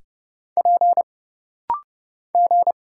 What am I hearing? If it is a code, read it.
Morse code, heard as a keyed tone, about 30 wpm, 700 Hz. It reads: P G
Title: Single Letters - Rapid-Fire 30wpm